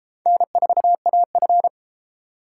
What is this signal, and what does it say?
Morse code, heard as a keyed tone, about 30 wpm, 700 Hz. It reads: N4AF